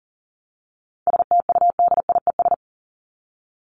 Morse code, heard as a keyed tone, about 40 wpm, 700 Hz. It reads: STUDIES
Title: ES:S Suffix - Top 100 Words with Base Words 40wpm